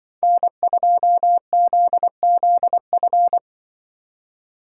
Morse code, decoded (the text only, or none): N2ZZF